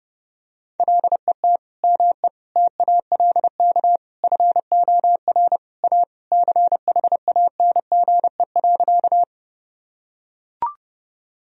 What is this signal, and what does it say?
Morse code, heard as a keyed tone, about 30 wpm, 700 Hz. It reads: LET ME TALK FOR A CHANGE.
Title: Sentences from Top 1000 Words - Review Part 1 - 30wpm